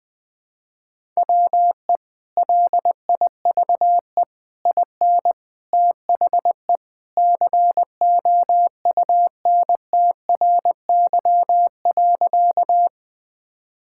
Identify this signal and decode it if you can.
Morse code, heard as a keyed tone, about 20 wpm, 700 Hz. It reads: WE LIVE IN THE COUNTRY.
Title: Sentences with 3-Word Phrases - Spoken Beginning - 20wpm